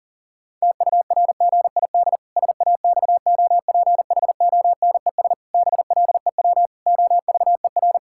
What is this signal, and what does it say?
Morse code, heard as a keyed tone, about 40 wpm, 700 Hz. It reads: TURGID SAXOPHONES BLEW OVER
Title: Pangram Challenge v2 - 40wpm